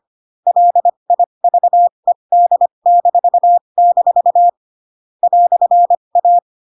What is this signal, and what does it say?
Morse code, heard as a keyed tone, about 25 wpm, spaced as for 22 wpm, 700 Hz. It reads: LIVED-- "A